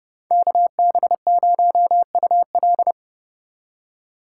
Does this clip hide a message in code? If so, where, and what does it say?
Yes: Morse code from 0.3 to 2.9 s, heard as a keyed tone, about 34 wpm, 700 Hz. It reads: KB0UL